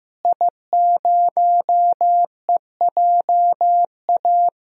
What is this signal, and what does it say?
Morse code, heard as a keyed tone, about 15 wpm, 700 Hz. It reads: I0EJA
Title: DXCC-100 Call Signs - 5 Characters - Repeated 3x - 15wpm